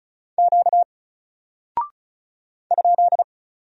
Morse code, heard as a keyed tone, about 36 wpm, 700 Hz. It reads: Q ?
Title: Single Character (Letter:Number:Punctuation:Prosign) Alt 35wpm